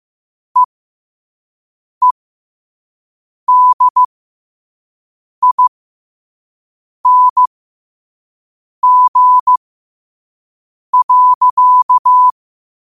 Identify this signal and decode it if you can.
Morse code, heard as a keyed tone, about 15 wpm, spaced as for 5 wpm, 1000 Hz. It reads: EEDING.
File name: qotdfiles / 5wpm